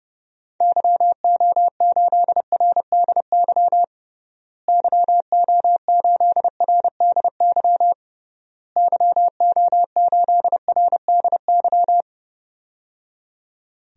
Morse code, heard as a keyed tone, about 30 wpm, 700 Hz. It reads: YO8RDY YO8RDY YO8RDY
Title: DXCC-100 Call Signs - 6 Characters - Repeated 3x - 30wpm